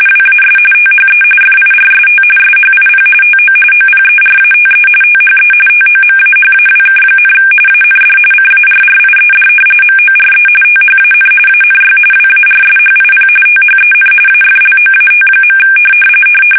RTTY, about 45.45 baud. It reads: WAS THE FIRST SHIP TO OPERATE THE TOMAHAWK, AND CARRIED MORE THAN ANY WARSHIP THEN AFLOAT.
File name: fsk850 / battleshipnj